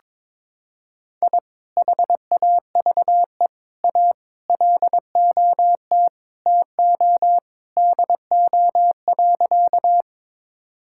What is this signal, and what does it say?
Morse code, heard as a keyed tone, about 21 wpm, 700 Hz. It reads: I HAVE A LOT TO DO.
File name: practice / 22wpm / Sentences with 2-Word Phrases - 22wpm